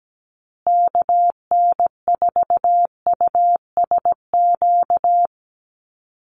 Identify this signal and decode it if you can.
Morse code, heard as a keyed tone, about 17 wpm, 700 Hz. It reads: KN4USQ
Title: USA Call Signs Repeated 3x 17wpm